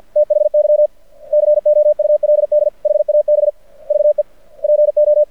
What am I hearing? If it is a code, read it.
Morse code, heard as a keyed tone, about 53 wpm, 600 Hz. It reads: THY COULD HAB VE PO
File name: archived / CRSnet